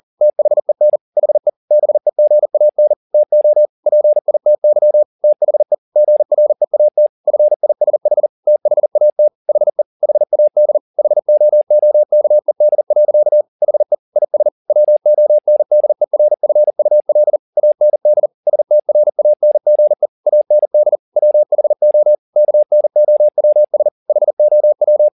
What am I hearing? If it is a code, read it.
Morse code, heard as a keyed tone, about 40 wpm, 600 Hz. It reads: THEN HE BEGAN TO PITY THE GREAT FISH THAT HE HAD HOOKED. HE IS WONDERFUL AND STRANGE AND WHO KNOWS HOW